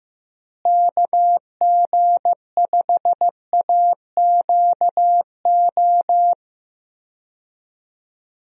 Morse code, heard as a keyed tone, about 15 wpm, 700 Hz. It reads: KG5AQO
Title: USA Call Signs - 6 Characters - Repeated 3x - 15wpm